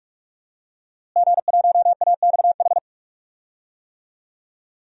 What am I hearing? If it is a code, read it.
Morse code, heard as a keyed tone, about 48 wpm, 700 Hz. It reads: G1AXH
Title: DXCC-100 Call Signs - 5 Characters - Repeated 2x - 45wpm